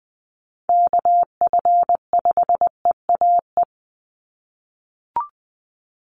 Morse code, heard as a keyed tone, about 20 wpm, 700 Hz. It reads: KF5EAE
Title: USA Call Signs 20wpm